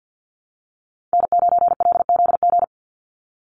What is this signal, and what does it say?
Morse code, heard as a keyed tone, about 50 wpm, 700 Hz. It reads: N9LZG
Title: USA Call Signs 50wpm